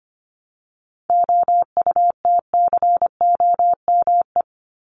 Morse code, heard as a keyed tone, about 25 wpm, 700 Hz. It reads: OUTCOME
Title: Top 1501-1750 Words 25wpm